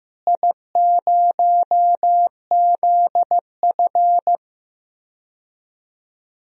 Morse code, heard as a keyed tone, about 15 wpm, 700 Hz. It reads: I0ZF
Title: DXCC-100 Call Signs - 4 Characters - Repeated 2x - 15wpm